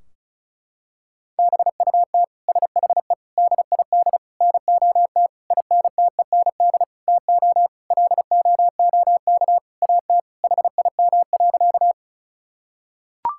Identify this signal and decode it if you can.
Morse code, heard as a keyed tone, about 36 wpm, 700 Hz. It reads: BUT SHE DID NOT INTEND TO LOOK AT HIM.
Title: Sentences from Top 1600 Words 35wpm